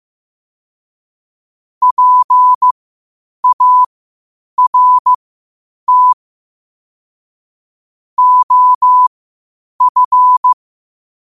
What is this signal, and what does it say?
Morse code, heard as a keyed tone, about 15 wpm, spaced as for 8 wpm, 1000 Hz. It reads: PART OF
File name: qotdfiles / 8wpm